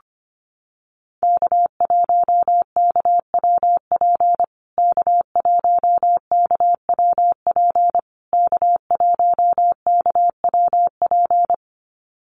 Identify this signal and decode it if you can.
Morse code, heard as a keyed tone, about 25 wpm, 700 Hz. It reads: K1KWP K1KWP K1KWP
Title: USA Call Signs Repeated 3x 25wpm